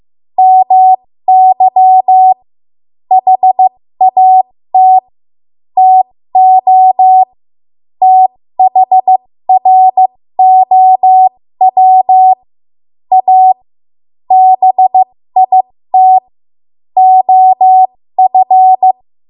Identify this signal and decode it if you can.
Morse code, heard as a keyed tone, about 15 wpm, spaced as for 13 wpm, 750 Hz. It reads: MY HAT TO THROW A BIT OF
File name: Morse / 13WPM